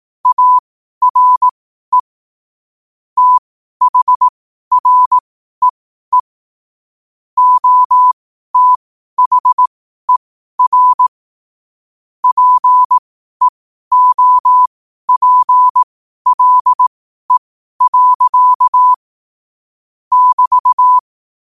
Morse code, heard as a keyed tone, about 18 wpm, spaced as for 12 wpm, 1000 Hz. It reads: ARE THREE OTHER PEOPLE. <BT>
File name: qotdfiles / 12wpm